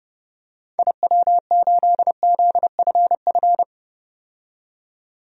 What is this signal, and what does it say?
Morse code, heard as a keyed tone, about 31 wpm, 700 Hz. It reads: IW8ZFF
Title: DXCC-100 Call Signs - 6 Characters - 30wpm